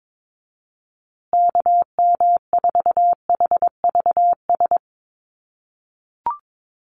Morse code, heard as a keyed tone, about 22 wpm, 700 Hz. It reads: KM4HVS
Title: USA Call Signs Repeated 2x 22wpm